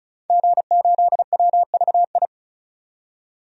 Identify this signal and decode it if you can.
Morse code, heard as a keyed tone, about 36 wpm, 700 Hz. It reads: G8WVI
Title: DXCC-100 Call Signs - 5 Characters - 35wpm